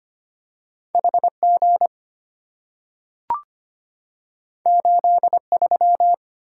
Morse code, heard as a keyed tone, about 25 wpm, 700 Hz. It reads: HG 83
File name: Two Letter-Number Combinations - 25wpm